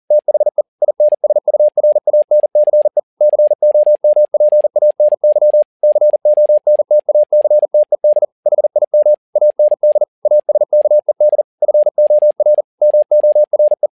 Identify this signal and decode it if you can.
Morse code, heard as a keyed tone, about 40 wpm, 600 Hz. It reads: THE INSURANCE COMPANY CONTACTED HIM AND ASKED FOR MORE
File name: learning center / tower trouble / tower touble 40wpm